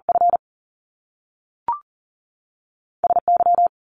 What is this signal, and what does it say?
Morse code, heard as a keyed tone, about 42 wpm, 700 Hz. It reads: F S Y
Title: Top Two Letter Combinations in Words 40wpm